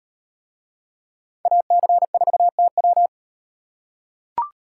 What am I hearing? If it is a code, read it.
Morse code, heard as a keyed tone, about 38 wpm, 700 Hz. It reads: AC4TW